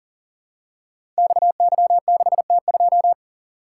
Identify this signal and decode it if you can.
Morse code, heard as a keyed tone, about 42 wpm, 700 Hz. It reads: XY6T2
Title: Code Groups - 5 Letter:Numbers - 40wpm